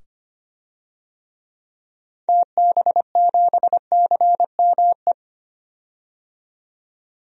Morse code, heard as a keyed tone, about 25 wpm, 690 Hz. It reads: TB7CME
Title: DXCC-100 Call Signs - 6 Characters - 25wpm